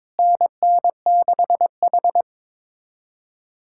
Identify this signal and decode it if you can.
Morse code, heard as a keyed tone, about 22 wpm, 700 Hz. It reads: NN6H
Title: USA Callsign DE Callsign - 4 Characters - 22wpm